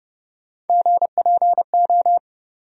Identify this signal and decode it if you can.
Morse code, heard as a keyed tone, about 32 wpm, 700 Hz. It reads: GPO
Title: USA Call Sign Postfixes - 3 Letters 30wpm